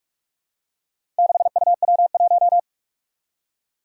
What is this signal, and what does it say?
Morse code, heard as a keyed tone, about 44 wpm, 700 Hz. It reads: 6UW1
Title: Four Challenging Letter-Number Combinations 45wpm